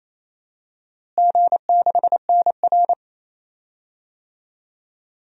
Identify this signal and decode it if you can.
Morse code, heard as a keyed tone, about 27 wpm, 700 Hz. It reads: G6NR
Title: DXCC-100 Call Signs - 4 Characters - Repeated 3x - 28wpm